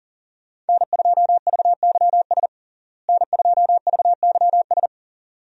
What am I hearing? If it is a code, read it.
Morse code, heard as a keyed tone, about 40 wpm, 700 Hz. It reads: N2VYS N2VYS
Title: USA Call Signs - 5 Characters - Repeated 2x - 40wpm